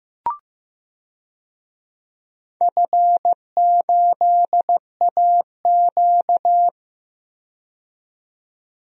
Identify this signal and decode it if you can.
Morse code, heard as a keyed tone, about 16 wpm, 700 Hz. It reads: F8AQ
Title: DXCC-100 Call Signs - 4 Characters - 15wpm